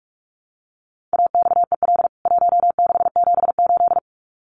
Morse code, heard as a keyed tone, about 44 wpm, 700 Hz. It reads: AXEL 1678
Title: CWT Exchange 45wpm